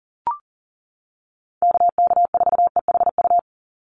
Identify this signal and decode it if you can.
Morse code, heard as a keyed tone, about 40 wpm, 700 Hz. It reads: KK4EHU
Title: USA Call Signs 40wpm